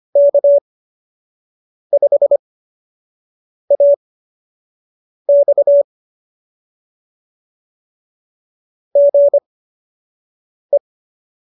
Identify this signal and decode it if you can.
Morse code, heard as a keyed tone, about 25 wpm, spaced as for 6 wpm, 570 Hz. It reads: K5AX GE